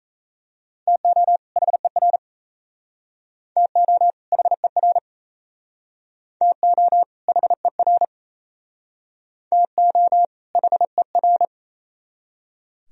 Morse code, heard as a keyed tone, about 35 wpm, 700 Hz. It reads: TO HER TO HER TO HER TO HER
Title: Speed-Racing - Sets of 2 Words 28wpm